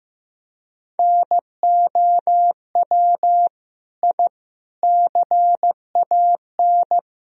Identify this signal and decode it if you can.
Morse code, heard as a keyed tone, about 15 wpm, 700 Hz. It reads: NOW I CAN
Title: Sentences from Top 1000 Words - Review Part 1 - 15wpm